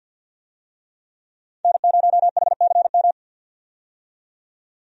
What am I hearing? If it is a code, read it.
Morse code, heard as a keyed tone, about 53 wpm, 700 Hz. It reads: N0HCM
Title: USA Call Signs - 5 Characters - 50wpm